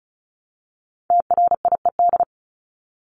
Morse code, heard as a keyed tone, about 34 wpm, 700 Hz. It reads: TRIED
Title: ED Suffix - Top 100 Words with Base Words 35wpm